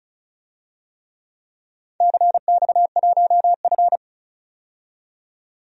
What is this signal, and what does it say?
Morse code, heard as a keyed tone, about 36 wpm, 700 Hz. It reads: CX1F
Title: DXCC-100 Call Signs - 4 Characters - 35wpm